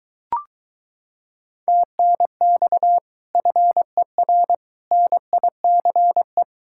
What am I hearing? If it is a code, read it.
Morse code, heard as a keyed tone, about 22 wpm, 700 Hz. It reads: TNX FER NICE